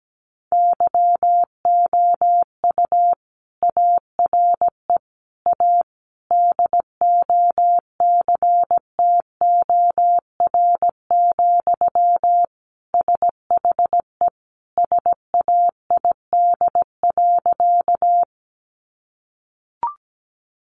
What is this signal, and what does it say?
Morse code, heard as a keyed tone, about 17 wpm, 700 Hz. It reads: YOU ARE A DOCTOR, SHE SAID.
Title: Sentences from Top 1500 Words 17wpm